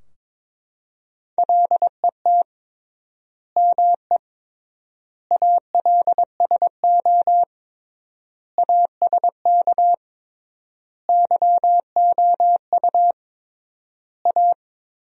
Morse code, heard as a keyed tone, about 22 wpm, 700 Hz. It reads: LET ME ALSO ASK YOU A
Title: 3x Word-Spacing - Sentences from Top 400 Words - 22wpm